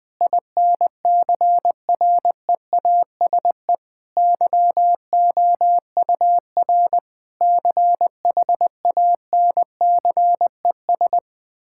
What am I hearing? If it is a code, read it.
Morse code, heard as a keyed tone, about 20 wpm, 700 Hz. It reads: INCREASE YOUR CHANCES
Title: Sentences with 8-Word Phrases - 20wpm